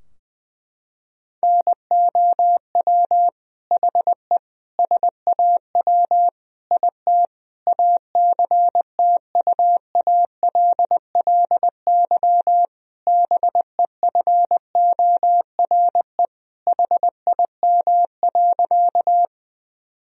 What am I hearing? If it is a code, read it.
Morse code, heard as a keyed tone, about 21 wpm, 700 Hz. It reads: NOW HE SAW IT ACTUALLY BEFORE HIM.
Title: Sentences from Top 1000 Words - Review Part 4 - 20wpm